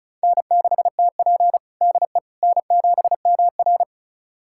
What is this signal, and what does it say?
Morse code, heard as a keyed tone, about 36 wpm, 700 Hz. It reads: N6TP DE N7MR
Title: USA Callsign DE Callsign - 4 Characters - 35wpm